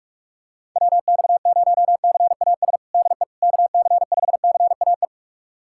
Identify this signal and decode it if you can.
Morse code, heard as a keyed tone, about 41 wpm, 700 Hz. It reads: WX0CAS DE KC5CAE
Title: USA Callsign DE Callsign - 6 Characters - 45wpm